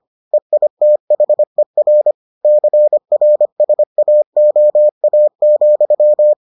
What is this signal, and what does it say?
Morse code, heard as a keyed tone, about 25 wpm, 600 Hz. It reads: EITHER CRSAOA,